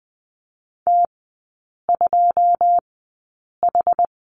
Morse code, heard as a keyed tone, about 20 wpm, spaced as for 8 wpm, 690 Hz. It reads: T2H
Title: 093 Lesson 16e - Letter c - Three Characters 20wpm